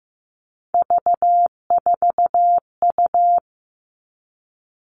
Morse code, heard as a keyed tone, about 15 wpm, 700 Hz. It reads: V4U
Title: All Cut Numbers - 3-Digit 15wpm